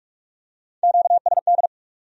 Morse code, heard as a keyed tone, about 39 wpm, 700 Hz. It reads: QSD